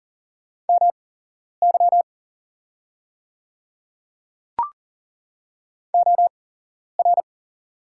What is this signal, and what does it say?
Morse code, heard as a keyed tone, about 40 wpm, spaced as for 11 wpm, 700 Hz. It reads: MY OR
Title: 8x Character Spacing - Top 100 Words - 40wpm